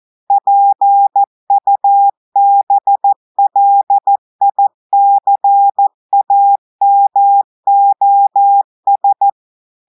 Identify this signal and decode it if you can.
Morse code, heard as a keyed tone, about 14 wpm, 800 Hz. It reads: PUBLICAMOS
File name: audios / portugais / portugais texte en clair 1 vitesse 14 mots minutes